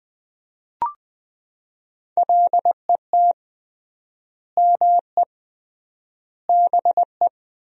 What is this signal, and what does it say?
Morse code, heard as a keyed tone, about 21 wpm, 700 Hz. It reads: LET ME BE
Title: 3x Word-Spacing - Sentences from Top 300 Words - 20wpm